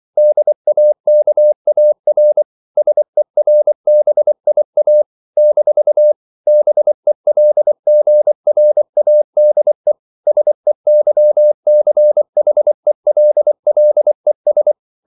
Morse code, heard as a keyed tone, about 24 wpm, 600 Hz. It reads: DAKAR SERBIA - BELGRADE SEYCHELLES
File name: CW / Practice / NationalCapitals24